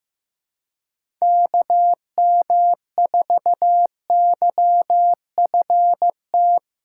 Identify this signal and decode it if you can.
Morse code, heard as a keyed tone, about 15 wpm, 700 Hz. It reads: KM4YFT
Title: USA Callsign DE Callsign - 6 Characters - 15wpm